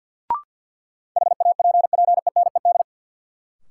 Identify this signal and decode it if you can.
Morse code, heard as a keyed tone, about 53 wpm, 700 Hz. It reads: HAPPENED
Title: Sentences from Top 1000 Words - Review Part 2 - New Word Drill - 50wpm